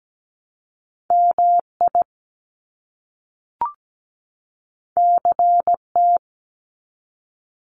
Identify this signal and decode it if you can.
Morse code, heard as a keyed tone, about 17 wpm, 700 Hz. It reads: MI CT
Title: US States 17wpm